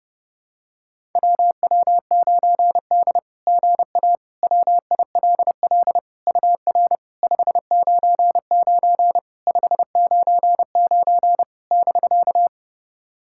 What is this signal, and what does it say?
Morse code, heard as a keyed tone, about 30 wpm, 700 Hz. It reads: WW9D GA WILL UR 599 599 <BK>
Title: SOTA QSO Practice 30wpm